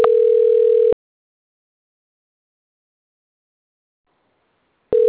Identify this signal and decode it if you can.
Ringback tone (North America), 440 + 480 Hz.
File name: out- - - - - .18191